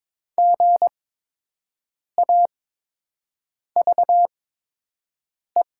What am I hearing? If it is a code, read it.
Morse code, heard as a keyed tone, about 22 wpm, spaced as for 6 wpm, 700 Hz. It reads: GAVE
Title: 8x Character Spacing - Top 500 Words - 22wpm